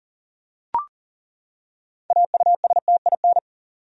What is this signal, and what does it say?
Morse code, heard as a keyed tone, about 39 wpm, 700 Hz. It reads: AUSTIN
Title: USA Common Names - 40wpm